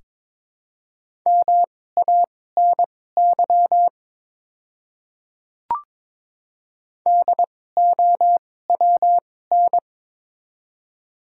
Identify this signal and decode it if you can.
Morse code, heard as a keyed tone, about 22 wpm, spaced as for 16 wpm, 700 Hz. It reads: MANY DOWN
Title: 2x Character Spacing - Top 100 Words - 22wpm